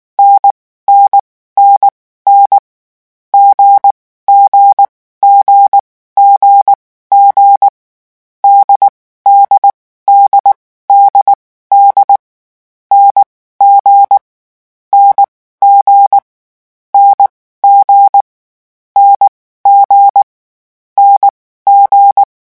Morse code, heard as a keyed tone, about 19 wpm, spaced as for 14 wpm, 800 Hz. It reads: NNNN G G G G G DD DDD N G NG N G N G NG
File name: morse mp3 / lecon4-vitesse 18